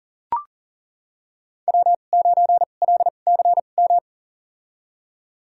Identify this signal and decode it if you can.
Morse code, heard as a keyed tone, about 40 wpm, spaced as for 29 wpm, 700 Hz. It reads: W9LCM
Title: USA Call Signs - 2x Character Spacing - 40wpm